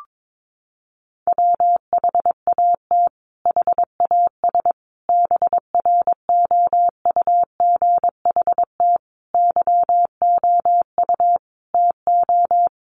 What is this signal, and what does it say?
Morse code, heard as a keyed tone, about 22 wpm, 700 Hz. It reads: WHAT HAS BROUGHT YOU TO
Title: Sentences from Top 1500 Words 22wpm